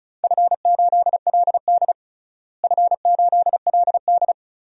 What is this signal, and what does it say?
Morse code, heard as a keyed tone, about 34 wpm, 700 Hz. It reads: F8LD F8LD
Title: DXCC-100 Call Signs - 4 Characters - Repeated 3x - 35wpm